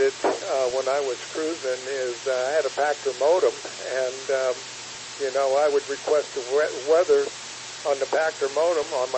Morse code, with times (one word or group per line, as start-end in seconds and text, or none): none